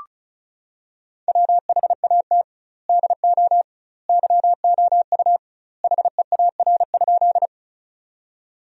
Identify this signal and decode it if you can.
Morse code, heard as a keyed tone, about 36 wpm, 700 Hz. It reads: WHAT DO YOU HEAR?
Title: 2x Word-Spacing - Sentences from Top 300 Words - 35wpm